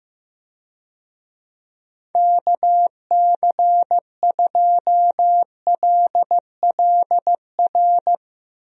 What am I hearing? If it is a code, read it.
Morse code, heard as a keyed tone, about 15 wpm, 700 Hz. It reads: KC2LLR
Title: USA Call Signs - 6 Characters - 15wpm